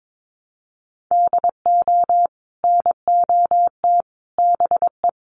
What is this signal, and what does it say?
Morse code, heard as a keyed tone, about 22 wpm, 700 Hz. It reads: DO NOT BE
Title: Sets of 5 Words Encore2 22wpm